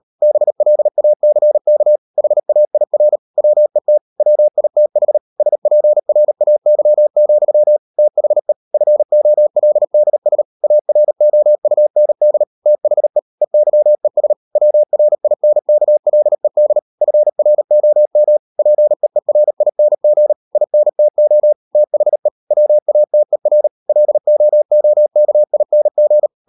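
Morse code, heard as a keyed tone, about 38 wpm, 600 Hz. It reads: BLACK HAIR WET WITH SPRAY, THE FOLDS AROUND THE EYES WRINKLED FROM PEERING INTO THE WATER LOOKING
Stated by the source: Morse code practice at 38 WPM, sending Ken Follett's novel Never in English, from CW med Gnister.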